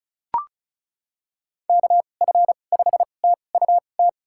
Morse code, heard as a keyed tone, about 36 wpm, spaced as for 25 wpm, 700 Hz. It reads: KF5TUT
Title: USA Call Signs - 6 Characters - 2x Character Spacing - 35wpm